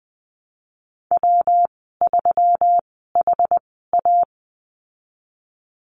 Morse code, heard as a keyed tone, about 20 wpm, spaced as for 14 wpm, 700 Hz. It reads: W3HA
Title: 179 Lesson 29g - Slash - Call Signs 20wpm